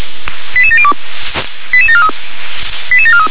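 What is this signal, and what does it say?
Five-tone selective call (ZVEI), heard as quick runs of five tones, tones of about 70 ms. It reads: 88702 88743 88743
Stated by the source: ZVEI five-tone selective call received by SDR.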